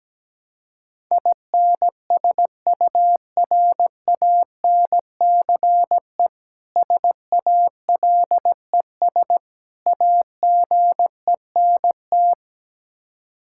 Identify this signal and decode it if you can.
Morse code, heard as a keyed tone, about 17 wpm, 700 Hz. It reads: INSURANCE SALES AGENT
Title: Practice QSO - Exchange 3 - More Info - 17wpm